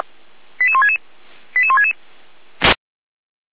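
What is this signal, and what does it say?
Five-tone selective call (ZVEI), heard as quick runs of five tones, tones of about 80 ms. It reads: 88166 88166
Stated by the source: ZVEI five-tone call 88352 received on SDR.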